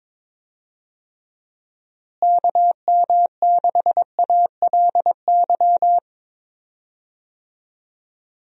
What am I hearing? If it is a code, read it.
Morse code, heard as a keyed tone, about 22 wpm, 700 Hz. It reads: KM6ALY
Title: USA Call Signs - 6 Characters - Repeated 2x - 22wpm